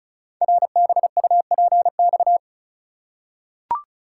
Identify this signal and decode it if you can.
Morse code, heard as a keyed tone, about 36 wpm, 700 Hz. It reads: RBUPX
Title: Code Groups - 5 Letters - 35wpm